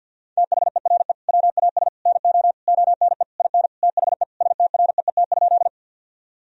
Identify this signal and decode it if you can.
Morse code, heard as a keyed tone, about 50 wpm, 700 Hz. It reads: THERE WAS NO ONE IN THE STREET?
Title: Sentences from Top 1000 Words - Review Part 3 - 50wpm